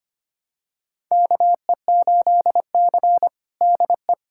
Morse code, heard as a keyed tone, about 25 wpm, 700 Hz. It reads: KE8C DE
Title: USA Callsign DE Callsign - 4 Characters - 25wpm